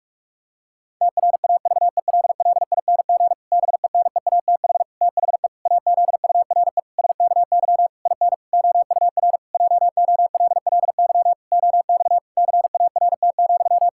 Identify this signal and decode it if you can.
Morse code, heard as a keyed tone, about 43 wpm, 700 Hz. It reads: TRAVELLING BENEATH THE AZURE SKY IN OUR JOLLY OX CART,
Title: Pangram Challenge v2 - 45wpm